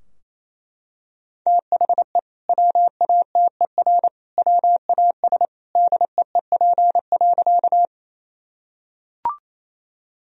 Morse code, heard as a keyed tone, about 28 wpm, 700 Hz. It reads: THE WATER WAS DEEP.
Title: Sentences from Top 1000 Words - Review Part 3 - 28wpm